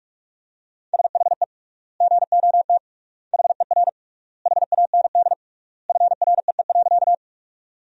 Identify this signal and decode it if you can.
Morse code, heard as a keyed tone, about 46 wpm, 700 Hz. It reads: SHE GOT HER HAND FREE.
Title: 3x Word-Spacing - Sentences from Top 500 Words - 45wpm